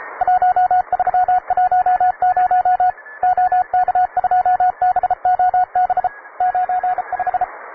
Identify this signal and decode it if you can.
Morse code, heard as a keyed tone, about 33 wpm, 730 Hz, with radio noise. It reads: 1310 OK2BOB 95
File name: wae / cw advanced / rk9jwv cw